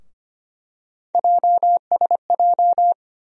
Morse code, heard as a keyed tone, about 25 wpm, 700 Hz. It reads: JSJ
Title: USA Call Sign Postfixes - 3 Letters 25wpm